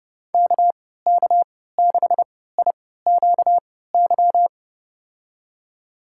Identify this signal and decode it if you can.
Morse code, heard as a keyed tone, about 30 wpm, spaced as for 17 wpm, 700 Hz. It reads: KK6IQY
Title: USA Call Signs - 6 Characters - 3x Character Spacing - 30wpm